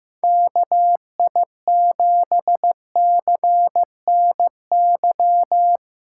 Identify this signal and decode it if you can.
Morse code, heard as a keyed tone, about 16 wpm, 700 Hz. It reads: KI7CNY
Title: USA Callsign DE Callsign 15wpm